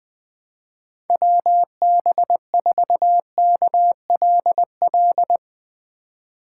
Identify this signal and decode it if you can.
Morse code, heard as a keyed tone, about 20 wpm, 700 Hz. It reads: WB4KLL
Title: USA Call Signs - 6 Characters - 2x Character Spacing - 20wpm